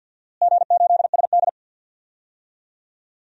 Morse code, heard as a keyed tone, about 51 wpm, 700 Hz. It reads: G8SD
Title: DXCC-100 Call Signs - 4 Characters - Repeated 2x - 50wpm